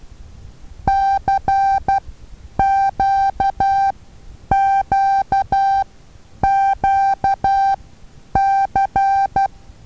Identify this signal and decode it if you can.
Morse code, heard as a keyed tone, about 12 wpm, spaced as for 9 wpm, 780 Hz. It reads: CQQQC
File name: Taviro HA6VA A